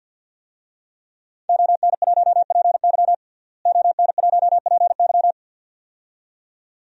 Morse code, heard as a keyed tone, about 52 wpm, 700 Hz. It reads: ON1PY ON1PY
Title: DXCC-100 Call Signs - 5 Characters - Repeated 2x - 50wpm